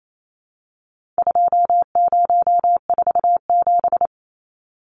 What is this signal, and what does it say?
Morse code, heard as a keyed tone, about 28 wpm, 700 Hz. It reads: 2047
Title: Numbers - 4-Digit 28wpm